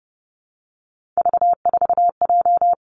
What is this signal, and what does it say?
Morse code, heard as a keyed tone, about 30 wpm, 700 Hz. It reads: V4J
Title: Three Challenging Letter-Number Combinations 30wpm